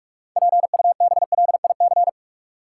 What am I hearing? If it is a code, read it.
Morse code, heard as a keyed tone, about 46 wpm, 700 Hz. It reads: PUBLIC
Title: Sets of 2 Words - New Word Drill - 45wpm